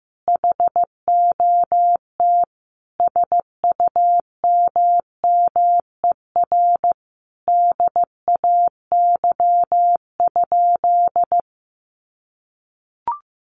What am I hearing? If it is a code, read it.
Morse code, heard as a keyed tone, about 15 wpm, 700 Hz. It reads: HOT SUMMER DAY?
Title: Sentences from Top 1500 Words 15wpm